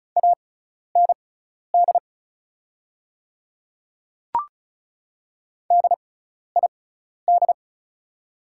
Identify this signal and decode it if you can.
Morse code, heard as a keyed tone, about 36 wpm, spaced as for 12 wpm, 690 Hz. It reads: AND DID
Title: 6x Character Spacing - Top 100 Words - 35wpm